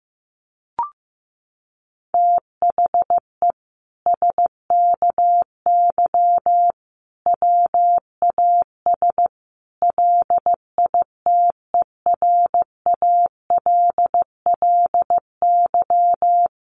Morse code, heard as a keyed tone, about 15 wpm, 700 Hz. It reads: THE SKY WAS LITERALLY
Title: Sentences from Top 1400 Words 15wpm